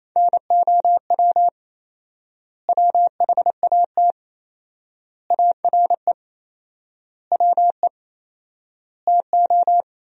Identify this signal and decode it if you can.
Morse code, heard as a keyed tone, about 27 wpm, 700 Hz. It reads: NOW WHAT ARE WE TO
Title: 4x Word-Spacing - Sentences from Top 100 Words - 28wpm